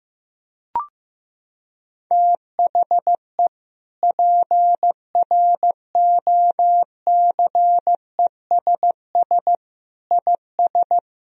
Morse code, heard as a keyed tone, about 15 wpm, 700 Hz. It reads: THE PROCESS IS
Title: Sentences from Top 1000 Words - Review Part 3 - 15wpm